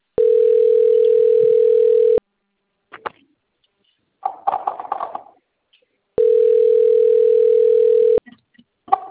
Ringback tone (North America), 440 + 480 Hz.